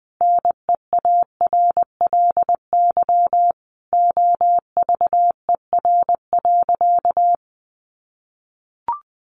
Morse code, heard as a keyed tone, about 20 wpm, 700 Hz. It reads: NEARLY OVER.
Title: Sentences from Top 1500 Words 20wpm